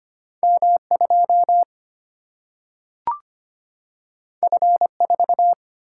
Morse code, heard as a keyed tone, about 25 wpm, 700 Hz. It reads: M 2 F 4
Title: Single-Letter Single-Number 25wpm